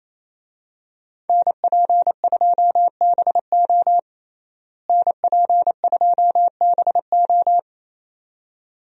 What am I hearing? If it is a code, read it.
Morse code, heard as a keyed tone, about 28 wpm, 700 Hz. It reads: NP2BO NP2BO
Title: USA Call Signs - 5 Characters - Repeated 2x - 28wpm